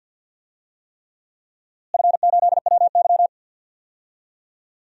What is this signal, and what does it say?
Morse code, heard as a keyed tone, about 51 wpm, 700 Hz. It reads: F8WY
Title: DXCC-100 Call Signs - 4 Characters - Repeated 3x - 50wpm